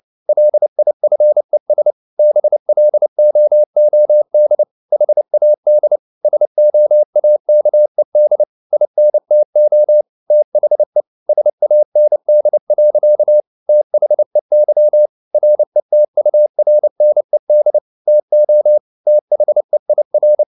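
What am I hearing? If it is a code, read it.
Morse code, heard as a keyed tone, about 29 wpm, 600 Hz. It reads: LIFES BLOOD HAD SOAKED INTO THE SAND. THEY RETURNED TO THEIR
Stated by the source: Morse code at 29 wpm, sending Ken Follett's novel Never in English, from CW med Gnister.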